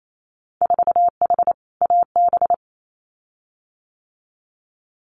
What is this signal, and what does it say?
Morse code, heard as a keyed tone, about 28 wpm, 690 Hz. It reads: 4H AB
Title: Winter Field Day Exchanges Encore 28wpm